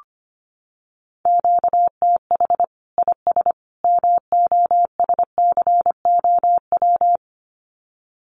Morse code, heard as a keyed tone, about 25 wpm, 700 Hz. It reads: QTH IS MOSCOW
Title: Common QSO Phrases 25wpm